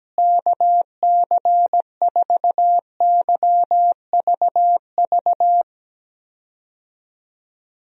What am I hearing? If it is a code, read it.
Morse code, heard as a keyed tone, about 17 wpm, 700 Hz. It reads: KC4YVV
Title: USA Call Signs - 6 Characters - Repeated 3x - 17wpm